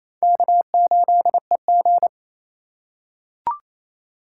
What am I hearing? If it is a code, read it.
Morse code, heard as a keyed tone, about 28 wpm, 700 Hz. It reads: K8EG
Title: USA Call Signs - 4 Characters - 2x Character Spacing - 28wpm